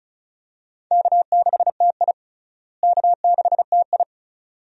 Morse code, heard as a keyed tone, about 34 wpm, 700 Hz. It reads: K6TI K6TI
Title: USA Call Signs - 4 Characters - Repeated 2x - 35wpm